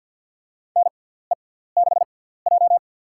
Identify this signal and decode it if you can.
Morse code, heard as a keyed tone, about 48 wpm, spaced as for 17 wpm, 700 Hz. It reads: NE6J
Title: USA Call Signs - 4 Characters - 6x Character Spacing - 50wpm